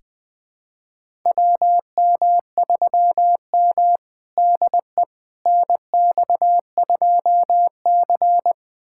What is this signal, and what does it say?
Morse code, heard as a keyed tone, about 20 wpm, 700 Hz. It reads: WM3M DE NX2C
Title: USA Callsign DE Callsign - 4 Characters - 20wpm